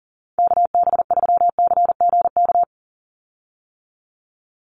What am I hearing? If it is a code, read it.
Morse code, heard as a keyed tone, about 40 wpm, 700 Hz. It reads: KB3CGK
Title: USA Call Signs 40wpm